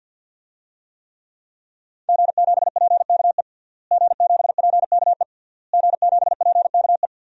Morse code, heard as a keyed tone, about 50 wpm, 700 Hz. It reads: G7PKE G7PKE G7PKE
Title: DXCC-100 Call Signs - 5 Characters - Repeated 3x - 50wpm